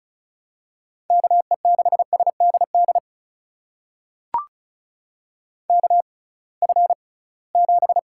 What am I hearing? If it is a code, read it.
Morse code, heard as a keyed tone, about 36 wpm, spaced as for 23 wpm, 700 Hz. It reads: KE6SDD KF7
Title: USA Call Signs - 6 Characters - 6x Character Spacing - 35wpm